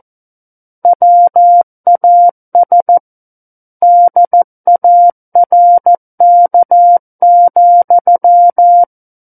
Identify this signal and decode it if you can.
Morse code, heard as a keyed tone, about 14 wpm, 700 Hz. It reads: WAS DARK,
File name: Book One / 05-Chapter 5 - 14 WPM